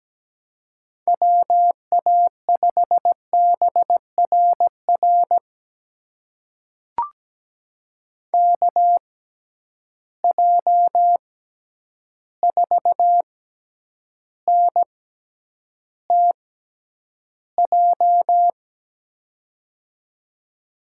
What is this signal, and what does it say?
Morse code, heard as a keyed tone, about 17 wpm, spaced as for 9 wpm, 700 Hz. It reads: WA5BRR KJ4NTJ
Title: USA Call Signs - 6x Character Spacing - 17wpm